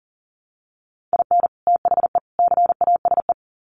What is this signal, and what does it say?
Morse code, heard as a keyed tone, about 40 wpm, 700 Hz. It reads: IN THE CASE
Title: Sets of 3 Words 40wpm